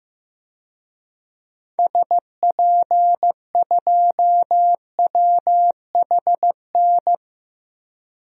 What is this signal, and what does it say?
Morse code, heard as a keyed tone, about 15 wpm, 700 Hz. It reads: SP2WHN
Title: DXCC-100 Call Signs - 6 Characters - Repeated 2x - 15wpm